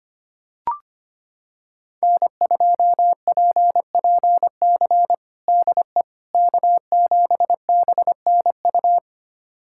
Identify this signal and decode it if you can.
Morse code, heard as a keyed tone, about 25 wpm, 700 Hz. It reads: N2PPC DE K7BNU
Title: USA Callsign DE Callsign - 5 Characters - 25wpm